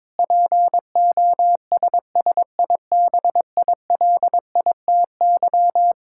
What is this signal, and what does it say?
Morse code, heard as a keyed tone, about 22 wpm, 700 Hz. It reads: POSSIBILITY
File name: practice / 22wpm / Sentences with 8-Word Phrases - 22wpm